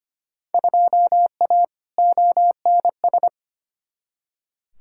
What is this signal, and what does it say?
Morse code, heard as a keyed tone, about 25 wpm, 700 Hz. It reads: 2A ONS
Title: Field Day Exchanges Encore - 25wpm